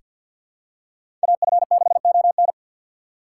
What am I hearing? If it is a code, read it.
Morse code, heard as a keyed tone, about 46 wpm, 700 Hz. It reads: AF6ON
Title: USA Call Signs - 5 Characters - 4x Character Spacing - 50wpm